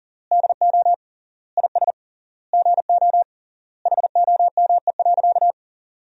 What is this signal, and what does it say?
Morse code, heard as a keyed tone, about 37 wpm, 700 Hz. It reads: DO IS GO HOME.
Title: 3x Word-Spacing - Sentences from Top 200 Words - 40wpm